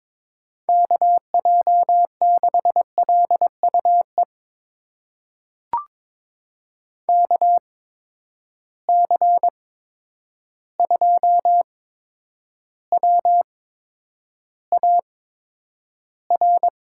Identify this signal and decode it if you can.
Morse code, heard as a keyed tone, about 22 wpm, spaced as for 9 wpm, 700 Hz. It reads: KJ6LUE KC2WAR
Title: USA Call Signs - 6 Characters - 8x Character Spacing - 22wpm